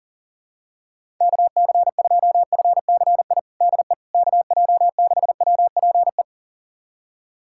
Morse code, heard as a keyed tone, about 40 wpm, 700 Hz. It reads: KC2FCI DE KJ6WPE
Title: USA Callsign DE Callsign 40wpm